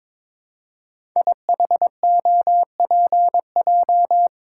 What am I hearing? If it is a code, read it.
Morse code, heard as a keyed tone, about 22 wpm, 700 Hz. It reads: IHOPJ
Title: Code Groups - 5 Letters - 22wpm